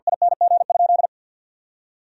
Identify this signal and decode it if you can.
Morse code, heard as a keyed tone, about 52 wpm, 700 Hz. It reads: ING?
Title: Sentences from Top 500 Words and Abbreviations 50wpm